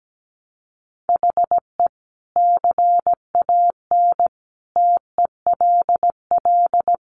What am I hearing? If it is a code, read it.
Morse code, heard as a keyed tone, about 17 wpm, 700 Hz. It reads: HE CAN TELL
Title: Sentences from Top 1300 Words 17wpm